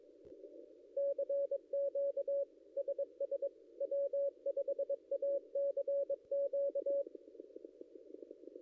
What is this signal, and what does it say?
Morse code, heard as a keyed tone, about 22 wpm, 560 Hz. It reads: CQ SS W5ACQ
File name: contests / ss-cw-k5tr-op-n5rz -0800